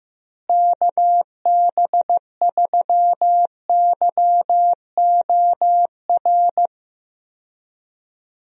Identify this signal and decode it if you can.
Morse code, heard as a keyed tone, about 15 wpm, 700 Hz. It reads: KB3YOR
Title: USA Call Signs - 6 Characters - Repeated 2x - 15wpm